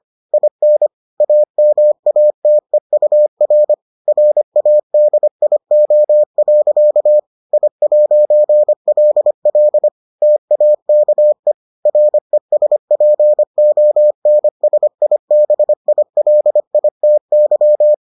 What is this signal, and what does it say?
Morse code, heard as a keyed tone, about 25 wpm, 600 Hz. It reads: IN AMATEUR RADIO. I'LL TAKE RESPONSIBILITY